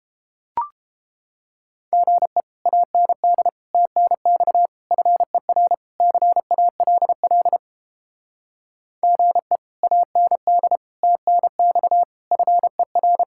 Morse code, heard as a keyed tone, about 29 wpm, 700 Hz. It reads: GE AND TNX FER CALL GE AND TNX FER